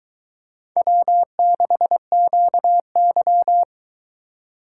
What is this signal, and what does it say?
Morse code, heard as a keyed tone, about 22 wpm, 700 Hz. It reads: W6QY